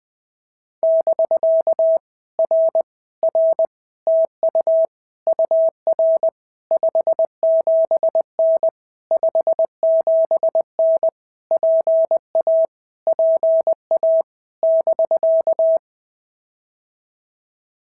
Morse code, heard as a keyed tone, about 20 wpm, 650 Hz. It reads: <BK> R R TU UR 57N 57N PA PA <BK>
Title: POTA QSO Practice - 20wpm